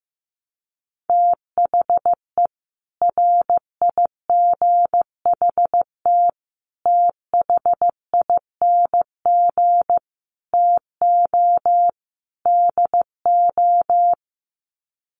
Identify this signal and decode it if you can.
Morse code, heard as a keyed tone, about 15 wpm, 700 Hz. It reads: THE RIGHT THING TO DO
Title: Sets of 5 Words 15wpm